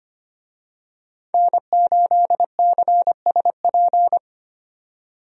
Morse code, heard as a keyed tone, about 25 wpm, 700 Hz. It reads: N8CSP
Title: USA Call Signs - 5 Characters - 25wpm